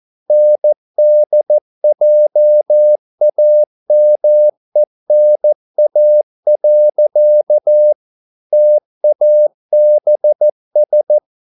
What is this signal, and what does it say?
Morse code, heard as a keyed tone, about 14 wpm, 600 Hz. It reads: NDJAMENA. TABS